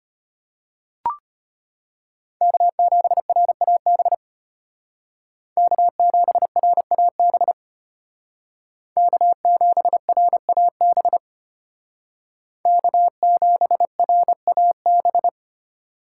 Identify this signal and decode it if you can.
Morse code, heard as a keyed tone, about 32 wpm, 700 Hz. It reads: K7RAB K7RAB K7RAB K7RAB